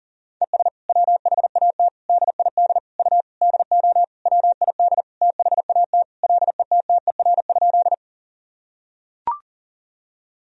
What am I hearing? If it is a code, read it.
Morse code, heard as a keyed tone, about 37 wpm, 700 Hz. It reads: ES WHAT DID U DO WID THAT LETTER?
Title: Sentences from Top 300 Words and Abbreviations 40wpm